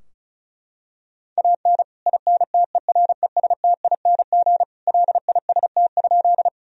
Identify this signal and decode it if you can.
Morse code, heard as a keyed tone, about 36 wpm, 700 Hz. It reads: AN INTERESTING LIST?
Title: Sentences from Top 1000 Words - Review Part 2 - 35wpm